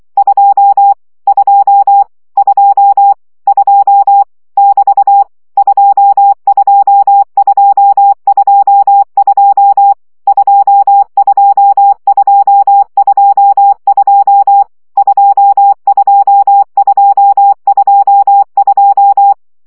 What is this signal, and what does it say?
Morse code, heard as a keyed tone, about 24 wpm, 800 Hz. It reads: 2 2 2 2 <BT> 22222 22222 22222
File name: CW gyakorlas 2 20 baud